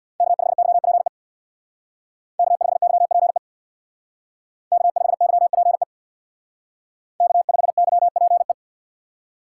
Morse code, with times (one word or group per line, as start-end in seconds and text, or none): 0.2-1.1 K5YPE
2.4-3.4 K5YPE
4.7-5.8 K5YPE
7.2-8.5 K5YPE